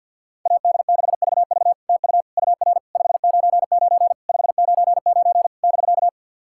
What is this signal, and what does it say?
Morse code, heard as a keyed tone, about 52 wpm, 690 Hz. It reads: AD6VV TU UR 599 599 <BK>
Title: SOTA QSO Practice 50wpm